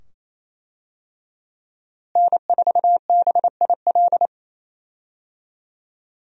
Morse code, heard as a keyed tone, about 28 wpm, 700 Hz. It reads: N4BIL
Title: USA Call Signs - 5 Characters - Repeated 2x - 28wpm